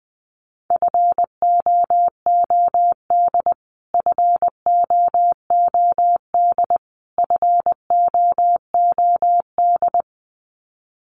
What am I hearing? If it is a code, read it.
Morse code, heard as a keyed tone, about 20 wpm, 700 Hz. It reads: FOOD FOOD FOOD
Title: Top 500 Words Repeated 20wpm